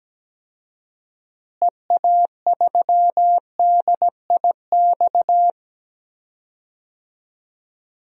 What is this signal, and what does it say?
Morse code, heard as a keyed tone, about 17 wpm, 700 Hz. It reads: EA3DIX
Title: DXCC-100 Call Signs - 6 Characters - Repeated 3x - 17wpm